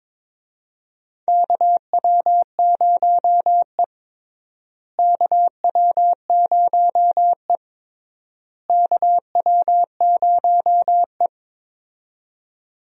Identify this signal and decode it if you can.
Morse code, heard as a keyed tone, about 22 wpm, 700 Hz. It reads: KW0E KW0E KW0E
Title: USA Call Signs - 4 Characters - Repeated 3x - 22wpm